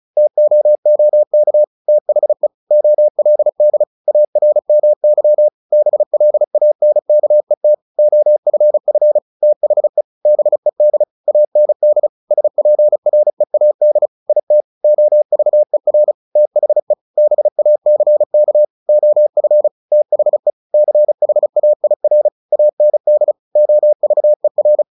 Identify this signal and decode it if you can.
Morse code, heard as a keyed tone, about 34 wpm, 600 Hz. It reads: TOOK THE OLD ARMY BLANKET OFF THE BED AND SPREAD IT OVER THE BACK OF THE CHAIR AND OVER
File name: the old man and the sea / the old man and the sea 1 35wpm